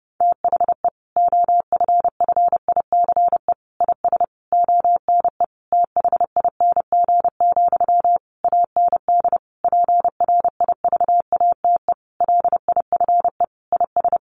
Morse code, heard as a keyed tone, about 30 wpm, 700 Hz. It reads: THE OFFICE IS ONE THING, AND PRIVATE LIFE IS
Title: Sentences from Top 1000 Words 30wpm